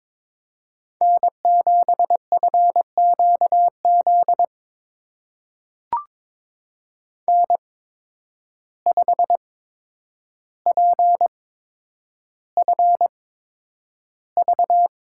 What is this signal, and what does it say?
Morse code, heard as a keyed tone, about 22 wpm, spaced as for 9 wpm, 700 Hz. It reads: N7FQZ N5PFV
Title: USA Call Signs - 5 Characters - 8x Character Spacing - 22wpm